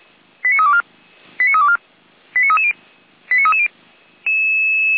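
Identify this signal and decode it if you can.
Five-tone selective call (ZVEI), heard as quick runs of five tones, tones of about 70 ms. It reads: 89324 89324 89330 89330
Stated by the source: ZVEI five-tone call received by SDR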